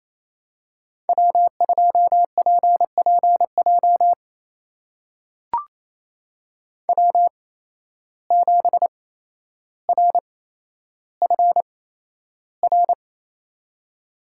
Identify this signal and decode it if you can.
Morse code, heard as a keyed tone, about 28 wpm, spaced as for 12 wpm, 700 Hz. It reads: W2PPJ W7RFR
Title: USA Call Signs - 8x Character Spacing - 28wpm